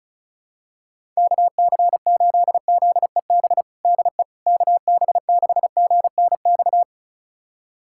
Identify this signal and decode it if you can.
Morse code, heard as a keyed tone, about 34 wpm, 700 Hz. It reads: KC8ZEB DE KB6GNX
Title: USA Callsign DE Callsign 35wpm